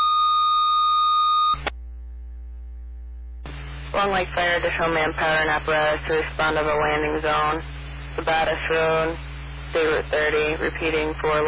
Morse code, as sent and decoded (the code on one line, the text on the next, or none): none